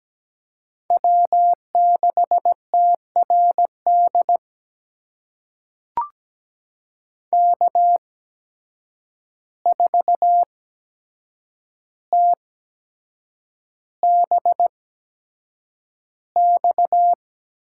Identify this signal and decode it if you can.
Morse code, heard as a keyed tone, about 17 wpm, spaced as for 7 wpm, 700 Hz. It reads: W6TRD K4TBX
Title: USA Call Signs - 5 Characters - 8x Character Spacing - 17wpm